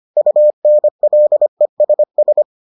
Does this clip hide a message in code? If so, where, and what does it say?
Yes: Morse code from 0.2 to 2.4 s, heard as a keyed tone, about 25 wpm, 600 Hz. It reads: UNLESS